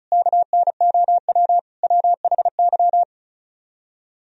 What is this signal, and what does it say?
Morse code, heard as a keyed tone, about 47 wpm, spaced as for 41 wpm, 700 Hz. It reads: KNOW WHY
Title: Speed-Racing - Sets of 4 Words 35wpm